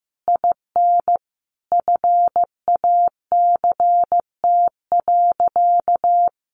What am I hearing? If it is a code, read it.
Morse code, heard as a keyed tone, about 15 wpm, 700 Hz. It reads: IN FACT.
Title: Sentences from Top 600 Words 15wpm